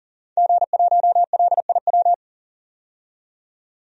Morse code, heard as a keyed tone, about 37 wpm, 700 Hz. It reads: G1LIW
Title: DXCC-100 Call Signs - 5 Characters - 40wpm